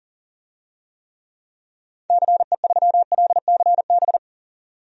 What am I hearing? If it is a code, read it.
Morse code, heard as a keyed tone, about 40 wpm, 700 Hz. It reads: CE3LCB
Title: DXCC-100 Call Signs - 6 Characters - 40wpm